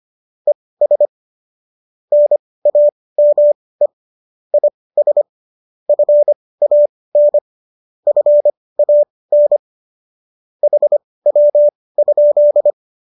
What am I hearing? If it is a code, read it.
Morse code, heard as a keyed tone, about 25 wpm, spaced as for 18 wpm, 590 Hz. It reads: ES NAME IS FAN FAN HW?
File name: qso203 18